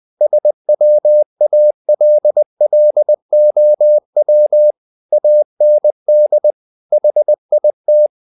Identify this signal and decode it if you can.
Morse code, heard as a keyed tone, about 20 wpm, 600 Hz. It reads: SWALLOW AND HIT
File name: the old man and the sea part 2 20wpm